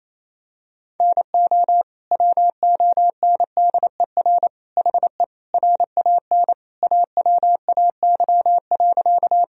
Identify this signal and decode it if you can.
Morse code, heard as a keyed tone, about 27 wpm, 700 Hz. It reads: NO WONDER HE RAN AWAY.
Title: Sentences from Top 1000 Words - Review Part 3 - 28wpm